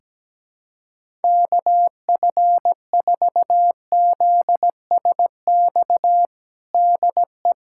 Morse code, heard as a keyed tone, about 17 wpm, 700 Hz. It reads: KF4ZSX DE
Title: USA Callsign DE Callsign 17wpm